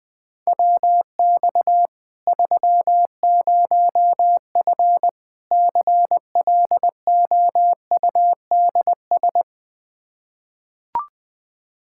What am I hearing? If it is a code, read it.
Morse code, heard as a keyed tone, about 21 wpm, 700 Hz. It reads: WX 30F CLOUDS
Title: Practice QSO - Exchange 2 - Info - 20wpm